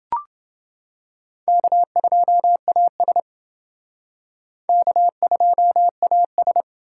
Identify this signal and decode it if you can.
Morse code, heard as a keyed tone, about 28 wpm, 700 Hz. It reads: K2AS K2AS